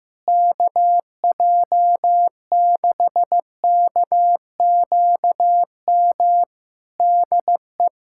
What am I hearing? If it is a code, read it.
Morse code, heard as a keyed tone, about 15 wpm, 700 Hz. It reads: KJ6KQM DE
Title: Practice QSO - Exchange 4 - Ending - 15wpm